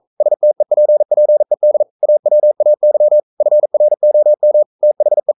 Morse code, heard as a keyed tone, about 42 wpm, 600 Hz. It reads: STEPPED AWAY FROM THE
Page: Romanen Never af Ken Follet på engelsk.